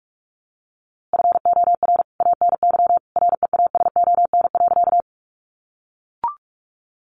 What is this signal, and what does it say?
Morse code, heard as a keyed tone, about 44 wpm, 700 Hz. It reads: FOR ANY REASON.
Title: A Sound Of Thunder 45wpm